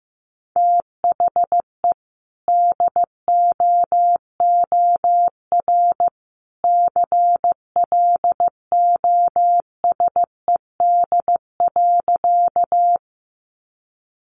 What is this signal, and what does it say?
Morse code, heard as a keyed tone, about 15 wpm, 700 Hz. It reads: THE DOOR CLOSED.
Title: Sentences from Top 1400 Words 15wpm